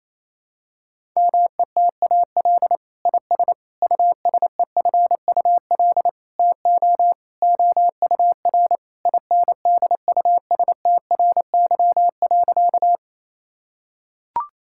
Morse code, heard as a keyed tone, about 28 wpm, 700 Hz. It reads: METAL IS USEFUL TO OUR INDUSTRY.
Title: Sentences from Top 1000 Words - Review Part 6 - 28wpm